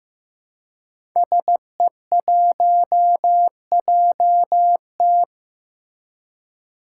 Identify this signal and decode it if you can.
Morse code, heard as a keyed tone, about 15 wpm, 700 Hz. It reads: SE1JT
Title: Five Challenging Letter-Number Combinations 15wpm